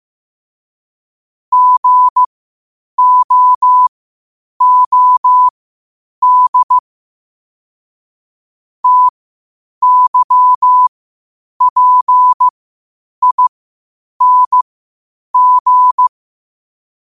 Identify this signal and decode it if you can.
Morse code, heard as a keyed tone, about 15 wpm, spaced as for 8 wpm, 1000 Hz. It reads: GOOD TYPING
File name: qotdfiles / 8wpm